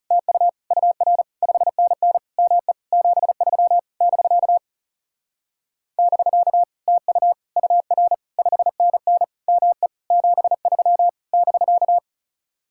Morse code, heard as a keyed tone, about 39 wpm, 700 Hz. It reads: TU UR 5NN ME 73 <BK> <BK> TU UR 5NN ME 73 <BK>